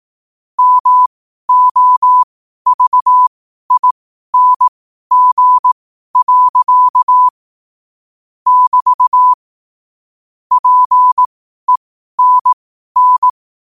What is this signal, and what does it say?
Morse code, heard as a keyed tone, about 18 wpm, spaced as for 12 wpm, 1000 Hz. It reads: MOVING. <BT> PENN
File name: qotdfiles / 12wpm